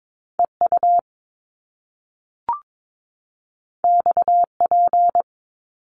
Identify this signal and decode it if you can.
Morse code, heard as a keyed tone, about 22 wpm, 700 Hz. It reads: EU XP
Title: Top Two Letter Combinations in Words 22wpm